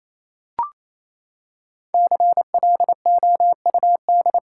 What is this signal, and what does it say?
Morse code, heard as a keyed tone, about 28 wpm, 700 Hz. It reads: CLOUD
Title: Five Letter Words - 28wpm